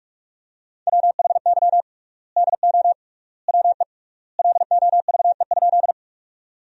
Morse code, heard as a keyed tone, about 46 wpm, 700 Hz. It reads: WHY DO WE LOVE?
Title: 3x Word-Spacing - Sentences from Top 400 Words - 45wpm